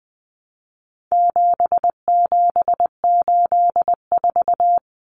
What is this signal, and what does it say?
Morse code, heard as a keyed tone, about 20 wpm, 700 Hz. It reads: 7784
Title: Common Cut Numbers - 4-Digit 20wpm